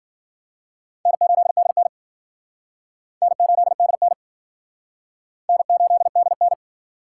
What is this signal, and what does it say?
Morse code, heard as a keyed tone, about 50 wpm, 700 Hz. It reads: N8DN N8DN N8DN